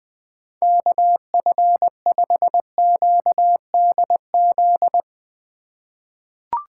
Morse code, heard as a keyed tone, about 20 wpm, 700 Hz. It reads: KF5QDZ
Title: USA Call Signs - 3x Character Spacing - 20wpm